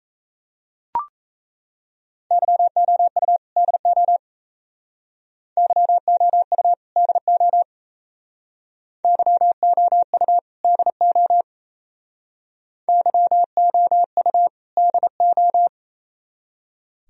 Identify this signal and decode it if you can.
Morse code, heard as a keyed tone, about 35 wpm, 700 Hz. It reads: YOU DO YOU DO YOU DO YOU DO